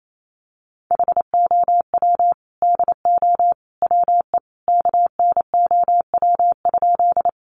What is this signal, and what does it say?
Morse code, heard as a keyed tone, about 28 wpm, 700 Hz. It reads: HOW DO WE KNOW?
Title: A Sound Of Thunder 28wpm